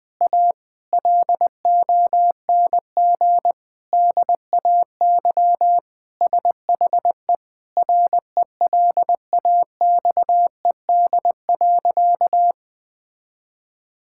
Morse code, heard as a keyed tone, about 20 wpm, 700 Hz. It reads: A LONG DAY SHE RELAXED.
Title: Sentences with 4-Word Phrases - 20wpm